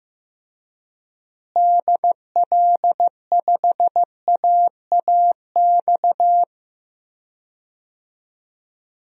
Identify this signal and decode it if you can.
Morse code, heard as a keyed tone, about 15 wpm, 700 Hz. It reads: DL5AAX
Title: DXCC-100 Call Signs - 6 Characters - Repeated 2x - 15wpm